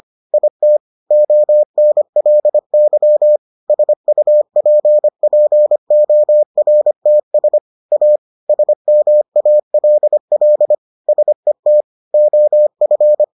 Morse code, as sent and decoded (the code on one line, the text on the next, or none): .. - / --- -. .-.. -.-- / ... ..- .--. .--. --- .-. - ... / .- / ... -- .- .-.. .-.. / ... - / --- ..-.
IT ONLY SUPPORTS A SMALL SET OF